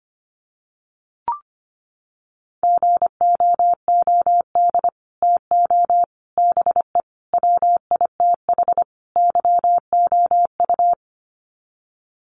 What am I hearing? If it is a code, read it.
Morse code, heard as a keyed tone, about 25 wpm, 700 Hz. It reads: GOOD TO BE WITH YOU
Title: Sets of 5 Words Encore2 25wpm